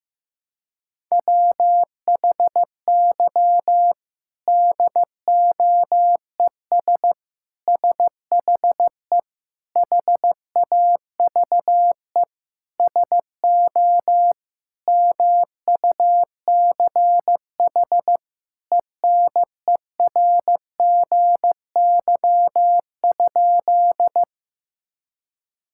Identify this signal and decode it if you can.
Morse code, heard as a keyed tone, about 15 wpm, 700 Hz. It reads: WHY DOES SHE HAVE SO MUCH ENERGY?
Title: Sentences from Top 1000 Words - Review Part 6 - 15wpm